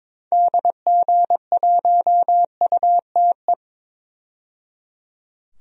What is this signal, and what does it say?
Morse code, heard as a keyed tone, about 21 wpm, 700 Hz. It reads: DG1UTE
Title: DXCC-100 Call Signs - 6 Characters - 22wpm